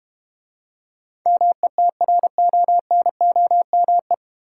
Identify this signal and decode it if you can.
Morse code, heard as a keyed tone, about 32 wpm, 700 Hz. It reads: METRONOME